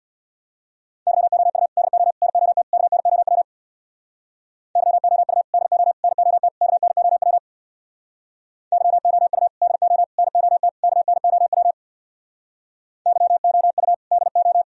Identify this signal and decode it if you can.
Morse code, heard as a keyed tone, about 65 wpm, 700 Hz. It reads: YOU DO NOT KNOW YOU DO NOT KNOW YOU DO NOT KNOW YOU DO